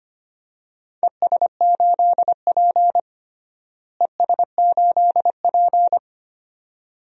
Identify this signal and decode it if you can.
Morse code, heard as a keyed tone, about 25 wpm, 700 Hz. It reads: ES8P ES8P
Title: DXCC-100 Call Signs - 4 Characters - Repeated 2x - 25wpm